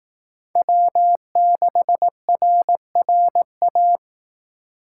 Morse code, heard as a keyed tone, about 18 wpm, 700 Hz. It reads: W6RRA
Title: Speed Racing - USA 1x3 Call Signs 15wpm